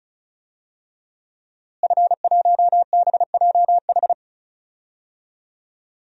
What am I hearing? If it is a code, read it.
Morse code, heard as a keyed tone, about 34 wpm, 700 Hz. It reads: F1BJH
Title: DXCC-100 Call Signs - 5 Characters - Repeated 2x - 35wpm